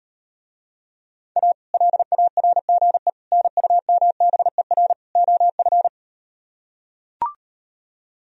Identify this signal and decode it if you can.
Morse code, heard as a keyed tone, about 39 wpm, 700 Hz. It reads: A LARGE NUMBER OF